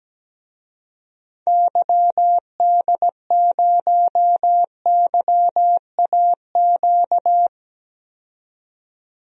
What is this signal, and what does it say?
Morse code, heard as a keyed tone, about 17 wpm, 700 Hz. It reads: YD0YAQ
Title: DXCC-100 Call Signs - 6 Characters - 17wpm